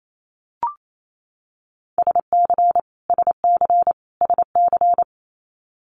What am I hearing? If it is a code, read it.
Morse code, heard as a keyed tone, about 28 wpm, 700 Hz. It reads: SC SC SC
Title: Multi-letter Phonemes Repeated 3x 28wpm